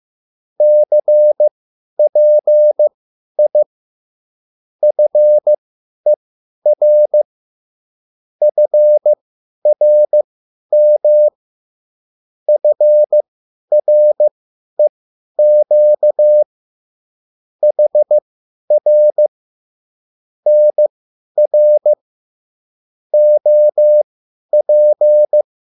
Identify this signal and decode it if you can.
Morse code, heard as a keyed tone, about 15 wpm, spaced as for 10 wpm, 600 Hz. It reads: CPI FER FRM FREQ HR NR OP
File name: QSL / CW lesson